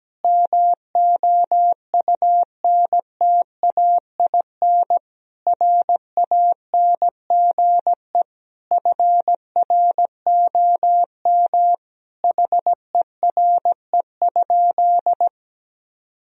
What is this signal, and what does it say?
Morse code, heard as a keyed tone, about 17 wpm, 700 Hz. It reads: MOUNTAIN RANGE FROM HERE?
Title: Sentences from Top 1000 Words - Review Part 4 - 17wpm